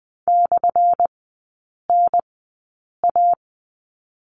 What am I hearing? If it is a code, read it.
Morse code, heard as a keyed tone, about 20 wpm, spaced as for 8 wpm, 700 Hz. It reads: /NA
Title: 178 Lesson 29e - Slash - Three Characters 20wpm